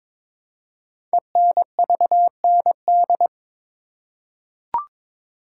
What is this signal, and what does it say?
Morse code, heard as a keyed tone, about 22 wpm, 700 Hz. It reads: ENVND
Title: Code Groups - 5 Letters - 22wpm